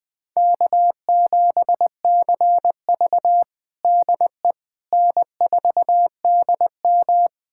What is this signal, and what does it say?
Morse code, heard as a keyed tone, about 20 wpm, 700 Hz. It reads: K7CV DE N4DM
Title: USA Callsign DE Callsign - 4 Characters - 20wpm